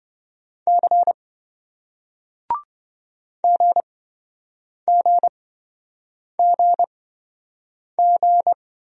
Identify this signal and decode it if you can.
Morse code, heard as a keyed tone, about 27 wpm, spaced as for 7 wpm, 700 Hz. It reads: C GGGG